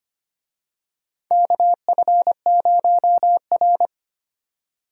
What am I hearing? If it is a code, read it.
Morse code, heard as a keyed tone, about 25 wpm, 700 Hz. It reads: KF0R
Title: USA Call Signs - 4 Characters - 25wpm